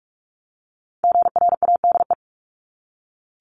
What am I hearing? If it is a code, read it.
Morse code, heard as a keyed tone, about 46 wpm, 700 Hz. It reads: GRADE
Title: Multi-letter Phonemes Repeated 3x 45wpm